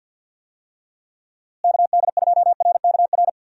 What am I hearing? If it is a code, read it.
Morse code, heard as a keyed tone, about 47 wpm, 700 Hz. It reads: KD2RKR
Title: USA Call Signs - 6 Characters - Repeated 2x - 50wpm